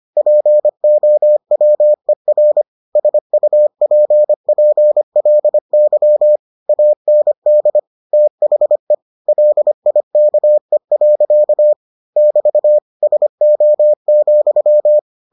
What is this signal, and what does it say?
Morse code, heard as a keyed tone, about 25 wpm, 600 Hz. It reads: POWER SUPPLY AND THE LIKE. <BT> SO,